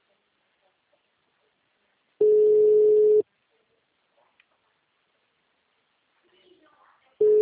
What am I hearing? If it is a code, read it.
Ringback tone (Europe, 425 Hz).
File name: out- -2041- - - .66